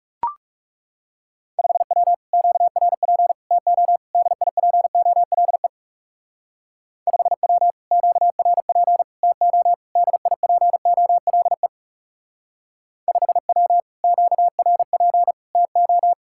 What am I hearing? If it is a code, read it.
Morse code, heard as a keyed tone, about 38 wpm, 700 Hz. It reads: 5W QRP TO DIPOLE 5W QRP TO DIPOLE 5W QRP TO